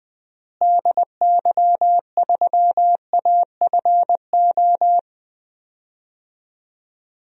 Morse code, heard as a keyed tone, about 20 wpm, 700 Hz. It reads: DY3AFO
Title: DXCC-100 Call Signs - 6 Characters - Repeated 2x - 20wpm